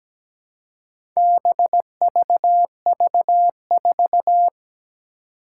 Morse code, heard as a keyed tone, about 17 wpm, 700 Hz. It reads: BVV4
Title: Four Challenging Letter-Number Combinations 17wpm